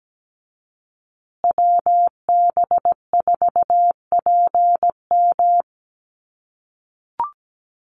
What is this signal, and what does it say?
Morse code, heard as a keyed tone, about 17 wpm, 700 Hz. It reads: WB4PM
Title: USA Call Signs Repeated 2x 17wpm